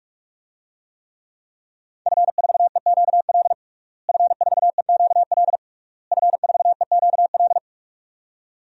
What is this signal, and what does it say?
Morse code, heard as a keyed tone, about 46 wpm, 700 Hz. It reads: F4EQL F4EQL F4EQL
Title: DXCC-100 Call Signs - 5 Characters - Repeated 3x - 45wpm